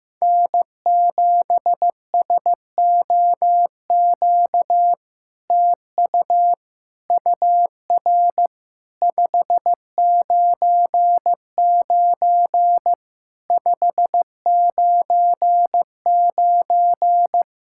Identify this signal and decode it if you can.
Morse code, heard as a keyed tone, about 16 wpm, 700 Hz. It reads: N7SOQ TU UR 599 599
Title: SOTA QSO Practice 15wpm